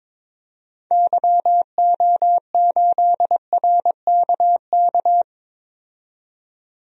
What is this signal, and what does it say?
Morse code, heard as a keyed tone, about 22 wpm, 700 Hz. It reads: YO8RKK
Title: DXCC-100 Call Signs - 6 Characters - 22wpm